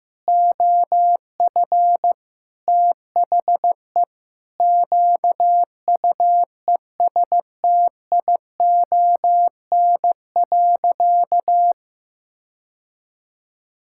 Morse code, heard as a keyed tone, about 15 wpm, 700 Hz. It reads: OF THE QUESTION.
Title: Sentences from Top 1000 Words - Review Part 5 - 15wpm